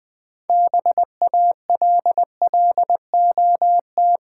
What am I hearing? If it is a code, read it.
Morse code, heard as a keyed tone, about 20 wpm, 700 Hz. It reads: BALLOT
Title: 6 Letter Words - First 3 Letters Given - 20wpm